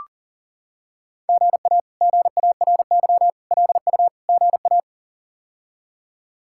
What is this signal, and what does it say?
Morse code, heard as a keyed tone, about 40 wpm, 700 Hz. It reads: GA GARY LU GA
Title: K1USN SST QSO Practice - 40wpm